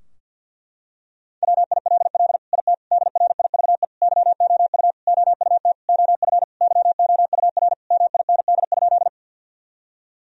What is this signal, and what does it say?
Morse code, heard as a keyed tone, about 51 wpm, 700 Hz. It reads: WILL IT DRIVE YOU OUT OF YOUR MIND?
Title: Sentences from Top 1000 Words - Review Part 2 - 50wpm